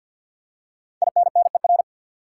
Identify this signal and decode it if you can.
Morse code, heard as a keyed tone, about 50 wpm, 700 Hz. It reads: INNER
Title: Five Letter Words - 50wpm